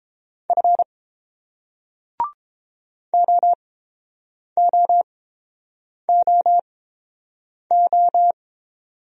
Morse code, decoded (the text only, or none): F OOOO